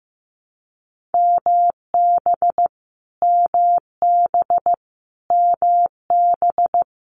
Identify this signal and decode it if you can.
Morse code, heard as a keyed tone, about 15 wpm, 700 Hz. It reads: MB MB MB
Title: Canadian Provinces and Territories - Repeated 3x - 15wpm